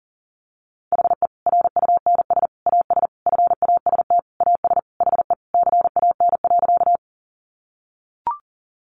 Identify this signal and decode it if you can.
Morse code, heard as a keyed tone, about 40 wpm, 700 Hz. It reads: HE RUNS AS FAST AS HE CAN.
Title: Additional Easy Phrases 40wpm